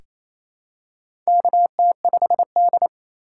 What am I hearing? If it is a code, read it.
Morse code, heard as a keyed tone, about 28 wpm, 700 Hz. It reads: KT5D
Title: USA Call Signs - 4 Characters - 2x Character Spacing - 28wpm